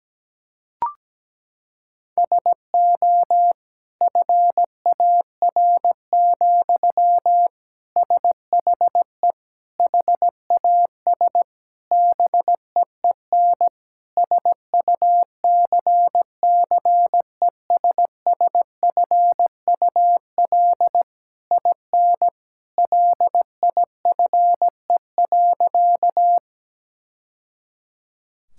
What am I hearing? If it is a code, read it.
Morse code, heard as a keyed tone, about 17 wpm, 700 Hz. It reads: SO FAR, SHE HAS BEEN SUCCESSFUL IN LIFE.
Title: Sentences from Top 1000 Words - Review Part 6 - 17wpm